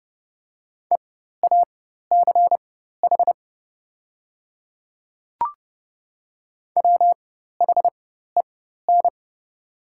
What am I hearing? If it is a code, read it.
Morse code, heard as a keyed tone, about 30 wpm, spaced as for 14 wpm, 700 Hz. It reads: EACH WHEN
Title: 4x Character Spacing - Top 100 Words - 30wpm